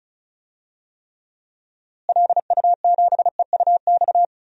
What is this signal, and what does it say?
Morse code, heard as a keyed tone, about 34 wpm, 700 Hz. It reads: LU7EUX
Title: DXCC-100 Call Signs - 6 Characters - Repeated 3x - 35wpm